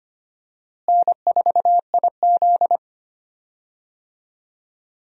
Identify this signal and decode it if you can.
Morse code, heard as a keyed tone, about 25 wpm, 700 Hz. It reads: N4IZ
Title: USA Call Signs - 4 Characters - Repeated 2x - 25wpm